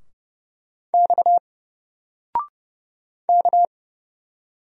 Morse code, heard as a keyed tone, about 31 wpm, 700 Hz. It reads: X K
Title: Single Letter-Number 30wpm